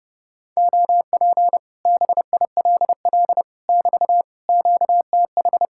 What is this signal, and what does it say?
Morse code, heard as a keyed tone, about 32 wpm, 700 Hz. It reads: OP BILL <BT> QTH